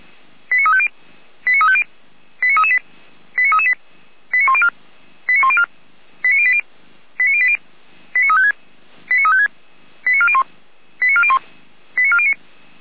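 Five-tone selective call (ZVEI), heard as quick runs of five tones, tones of about 70 ms. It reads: 89350 89350 89338 89338 89104 89104 89080 89080 89356 89356 89401 89401 89409
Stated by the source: ZVEI five-tone selective call received by SDR, from feuerwehr-orlamuende.org (Feuerwehr Orlamünde fire brigade).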